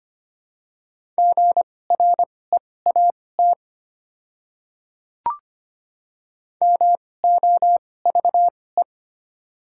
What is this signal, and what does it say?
Morse code, heard as a keyed tone, about 25 wpm, spaced as for 18 wpm, 700 Hz. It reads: GREAT MOVE
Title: 2x Character Spacing - Top 200 Words - 25wpm